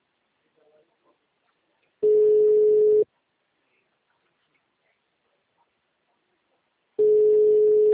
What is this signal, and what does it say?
Ringback tone (Europe, 425 Hz).
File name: out- -2004- - - .43125